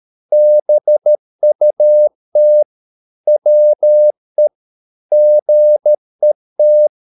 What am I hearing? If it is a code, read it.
Morse code, heard as a keyed tone, about 13 wpm, 600 Hz. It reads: BUT WE GET